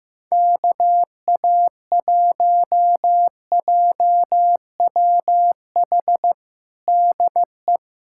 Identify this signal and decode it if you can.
Morse code, heard as a keyed tone, about 15 wpm, 700 Hz. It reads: KA1JWH DE
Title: USA Callsign DE Callsign - 6 Characters - 15wpm